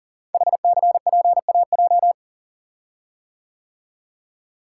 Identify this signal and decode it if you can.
Morse code, heard as a keyed tone, about 43 wpm, 700 Hz. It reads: HCPAJ
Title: Code Groups - 5 Letters - 40wpm